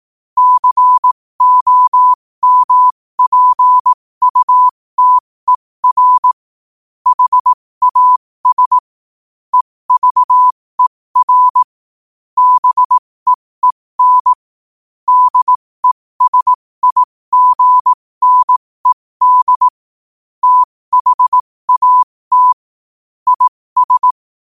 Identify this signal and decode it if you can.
Morse code, heard as a keyed tone, about 18 wpm, spaced as for 15 wpm, 1000 Hz. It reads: COMPUTER HAS EVER BEEN DESIGNED THAT IS
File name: qotdfiles / 15wpm